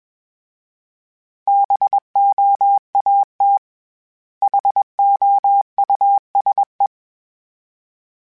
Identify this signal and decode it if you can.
Morse code, heard as a keyed tone, about 21 wpm, 800 Hz. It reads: BOAT HOUSE
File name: General / General-mp3s / Houses @20